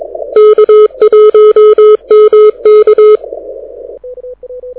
Morse code, heard as a keyed tone, about 22 wpm, 430 Hz. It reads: K1MK
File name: audio zawody / IARU 2015 CW SN7Q